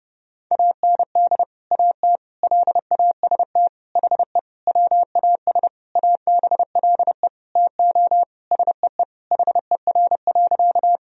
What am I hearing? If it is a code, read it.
Morse code, heard as a keyed tone, about 32 wpm, 700 Hz. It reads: AND AT LAST HE WAS ABLE TO SEE HER.
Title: Sentences from Top 1000 Words - Review Part 2 - 30wpm